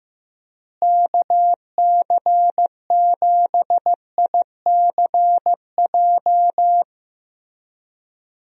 Morse code, heard as a keyed tone, about 16 wpm, 700 Hz. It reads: KC7ICJ
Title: USA Call Signs - 6 Characters - Repeated 3x - 15wpm